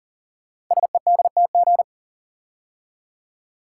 Morse code, heard as a keyed tone, about 39 wpm, 700 Hz. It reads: SEDTG
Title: Code Groups - 5 Letter:Numbers - 40wpm